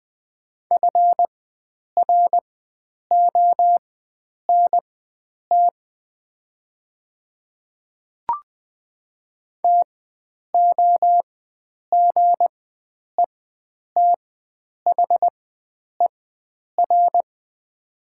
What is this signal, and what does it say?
Morse code, heard as a keyed tone, about 20 wpm, spaced as for 9 wpm, 700 Hz. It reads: FRONT TOGETHER
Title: 4x Character Spacing - Top 500 Words - 20wpm